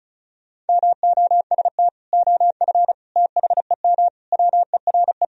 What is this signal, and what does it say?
Morse code, heard as a keyed tone, about 36 wpm, 700 Hz. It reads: MOST OF THEM WERE
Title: Sentences from Top 1000 Words - Review Part 1 - 35wpm